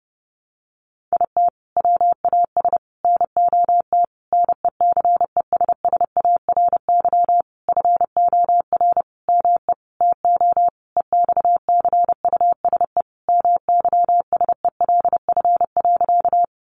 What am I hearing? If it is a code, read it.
Morse code, heard as a keyed tone, about 30 wpm, 700 Hz. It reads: IT WAS NOT NECESSARY FOR ME TO EXCUSE MYSELF.
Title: Sentences from Top 1300 Words 30wpm